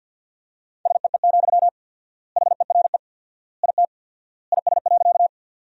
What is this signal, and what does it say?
Morse code, heard as a keyed tone, about 50 wpm, 700 Hz. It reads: SEE, HERE IT IS.
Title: 4x Word-Spacing - Sentences from Top 200 Words - 50wpm